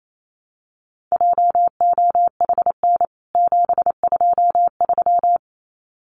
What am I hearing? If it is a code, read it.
Morse code, heard as a keyed tone, about 28 wpm, 700 Hz. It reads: JOHN 723
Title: CWT Exchange 28wpm